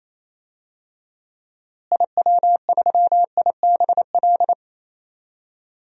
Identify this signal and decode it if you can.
Morse code, heard as a keyed tone, about 28 wpm, 700 Hz. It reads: IW3IBL
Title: DXCC-100 Call Signs - 6 Characters - Repeated 3x - 28wpm